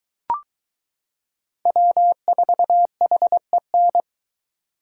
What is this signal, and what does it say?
Morse code, heard as a keyed tone, about 22 wpm, 700 Hz. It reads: W4HEN